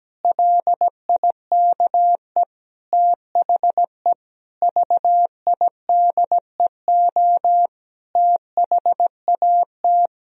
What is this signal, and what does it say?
Morse code, heard as a keyed tone, about 17 wpm, 700 Hz. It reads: LIKE THE VIDEO THAT
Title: Sentences from Top 1000 Words - Review Part 4 - 17wpm